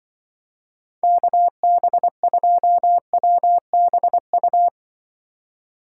Morse code, heard as a keyed tone, about 25 wpm, 700 Hz. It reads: KB2WBU
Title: Speed Racing - USA Call Signs 20wpm